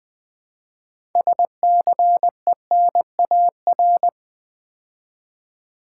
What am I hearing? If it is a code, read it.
Morse code, heard as a keyed tone, about 20 wpm, 700 Hz. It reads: SCENAR
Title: Build A Word - 2 Letters At A Time - 8-Letter Words - 20wpm